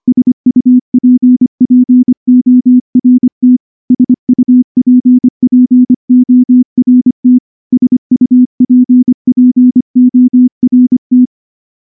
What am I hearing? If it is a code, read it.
Morse code, heard as a keyed tone, about 24 wpm, 250 Hz. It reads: SUPPORT SUPPORT SUPPORT
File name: morse-training-25WPM